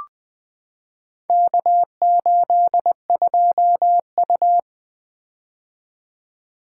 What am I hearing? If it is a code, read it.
Morse code, heard as a keyed tone, about 20 wpm, 700 Hz. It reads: K82U
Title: Four Letter-Number Combinations - 20wpm